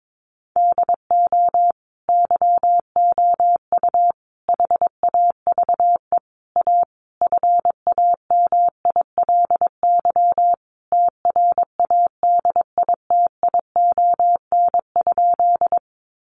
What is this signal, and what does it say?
Morse code, heard as a keyed tone, about 22 wpm, 700 Hz. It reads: DO YOU HAVE A FAMILY TRADITION?
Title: Sentences from Top 1500 Words 22wpm